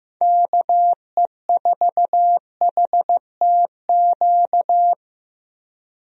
Morse code, heard as a keyed tone, about 15 wpm, 700 Hz. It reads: KE4HTQ
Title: USA Call Signs - 6 Characters - Repeated 2x - 15wpm